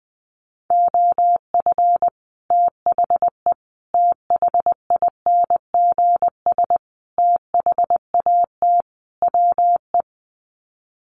Morse code, heard as a keyed tone, about 20 wpm, 700 Hz. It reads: OF THE THINGS THAT WE
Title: Sets of 5 Words Encore 20wpm